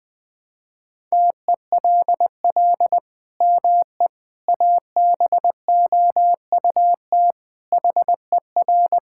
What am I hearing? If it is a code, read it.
Morse code, heard as a keyed tone, about 21 wpm, 700 Hz. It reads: TELL ME ABOUT HER
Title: Sentences from Top 1000 Words - Review Part 5 - 20wpm